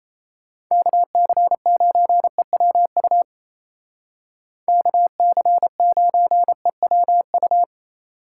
Morse code, heard as a keyed tone, about 36 wpm, 700 Hz. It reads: KC9EWU KC9EWU